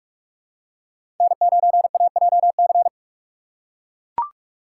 Morse code, heard as a keyed tone, about 42 wpm, 700 Hz. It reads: N9AJC
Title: USA Call Signs - 5 Characters - 4x Character Spacing - 45wpm